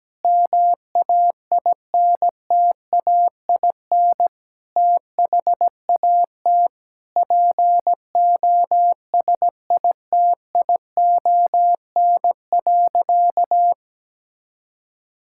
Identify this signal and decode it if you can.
Morse code, heard as a keyed tone, about 17 wpm, 700 Hz. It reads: MAINTAIN THAT POSITION.
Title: Sentences from Top 1000 Words - Review Part 5 - 17wpm